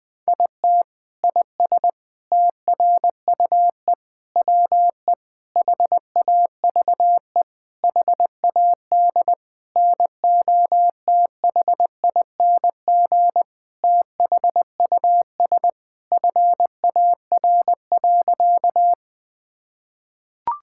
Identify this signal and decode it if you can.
Morse code, heard as a keyed tone, about 20 wpm, 700 Hz. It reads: IT IS TRUE WE HAVE HAD NOTHING THUS FAR.
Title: Sentences from Top 1000 Words - Review Part 5 - 20wpm